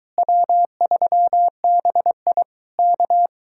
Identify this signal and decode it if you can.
Morse code, heard as a keyed tone, about 23 wpm, 700 Hz. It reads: W3BI K